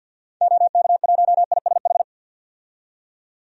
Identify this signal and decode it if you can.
Morse code, heard as a keyed tone, about 50 wpm, 700 Hz. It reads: OK1ISH
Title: DXCC-100 Call Signs - 6 Characters - 50wpm